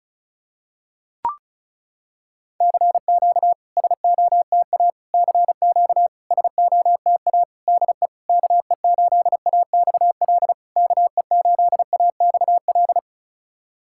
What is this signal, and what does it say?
Morse code, heard as a keyed tone, about 36 wpm, 700 Hz. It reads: CQ SOTA CQ SOTA DE KE8AXL KE8AXL
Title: SOTA QSO Practice 35wpm